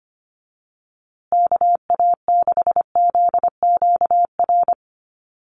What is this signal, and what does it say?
Morse code, heard as a keyed tone, about 25 wpm, 700 Hz. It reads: KA6ZQR
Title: USA Call Signs Repeated 3x 25wpm